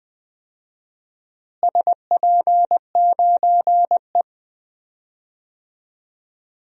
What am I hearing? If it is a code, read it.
Morse code, heard as a keyed tone, about 20 wpm, 700 Hz. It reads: SP9E
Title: DXCC-100 Call Signs - 4 Characters - Repeated 3x - 20wpm